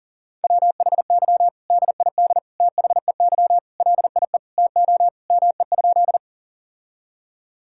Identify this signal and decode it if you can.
Morse code, heard as a keyed tone, about 37 wpm, 700 Hz. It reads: WHY DID THEY LIE TO ME?
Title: Sentences from Top 1600 Words 40wpm